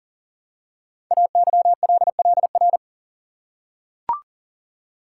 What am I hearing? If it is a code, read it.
Morse code, heard as a keyed tone, about 42 wpm, 690 Hz. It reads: AYLLR
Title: Code Groups - 5 Letters - 40wpm